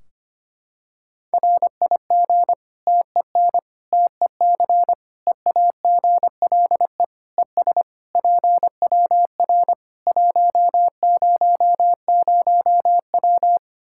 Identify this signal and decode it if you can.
Morse code, heard as a keyed tone, about 25 wpm, 700 Hz. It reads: RIG TEN TEC EAGLE ES PWR 100W
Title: Practice QSO - Exchange 2 - Info - 25wpm